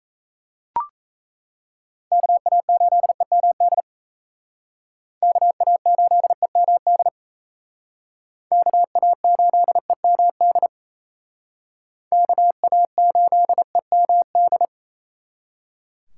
Morse code, heard as a keyed tone, about 35 wpm, 700 Hz. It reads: KA8EMD KA8EMD KA8EMD KA8EMD